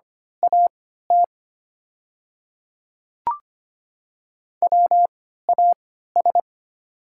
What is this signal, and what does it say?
Morse code, heard as a keyed tone, about 25 wpm, spaced as for 14 wpm, 690 Hz. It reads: AT WAS
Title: 3x Character Spacing - Top 100 Words - 25wpm